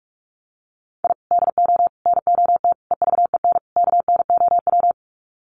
Morse code, heard as a keyed tone, about 44 wpm, 700 Hz. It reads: I DO NOT EVEN KNOW
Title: Sets of 5 Words 45wpm